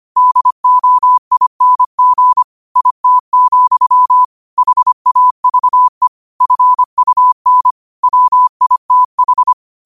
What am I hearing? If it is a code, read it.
Morse code, heard as a keyed tone, about 25 wpm, 1000 Hz. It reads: DOING IT, HAVE FUN WITH
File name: qotdfiles / 25wpm